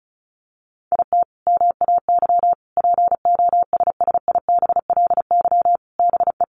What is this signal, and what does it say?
Morse code, heard as a keyed tone, about 34 wpm, 700 Hz. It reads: IT MAY POSSIBLY BE
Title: Sentences from Top 1300 Words 35wpm